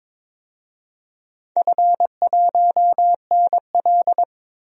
Morse code, heard as a keyed tone, about 22 wpm, 700 Hz. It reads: F1NL
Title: DXCC-100 Call Signs - 4 Characters - 22wpm